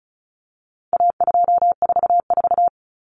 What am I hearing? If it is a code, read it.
Morse code, heard as a keyed tone, about 34 wpm, 700 Hz. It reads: A244
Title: Common Cut Numbers - 4-Digit 35wpm